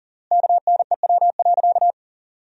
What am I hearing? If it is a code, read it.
Morse code, heard as a keyed tone, about 40 wpm, 700 Hz. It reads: KNEW.
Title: 3x Word-Spacing - Sentences from Top 400 Words - 40wpm